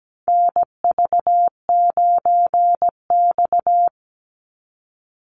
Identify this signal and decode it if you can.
Morse code, heard as a keyed tone, about 17 wpm, 700 Hz. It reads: NV9X
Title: Call Signs Repeated 3x - 17wpm